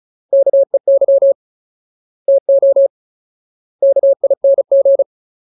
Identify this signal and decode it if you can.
Morse code, heard as a keyed tone, about 34 wpm, 550 Hz. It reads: KEY TO KING
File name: Ultimate Word List / 35WPM Ultimate